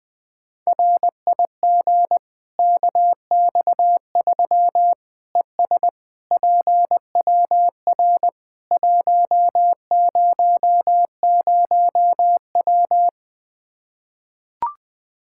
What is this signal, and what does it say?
Morse code, heard as a keyed tone, about 21 wpm, 700 Hz. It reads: RIG KX3 ES PWR 100W
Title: Practice QSO - Exchange 2 - Info - 20wpm